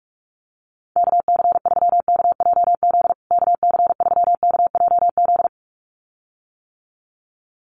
Morse code, heard as a keyed tone, about 44 wpm, 700 Hz. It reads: KC3KJZ KC3KJZ
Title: USA Call Signs Repeated 2x 45wpm